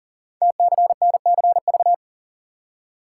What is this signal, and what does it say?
Morse code, heard as a keyed tone, about 42 wpm, 700 Hz. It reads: TCNCV
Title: Code Groups - 5 Letters - 40wpm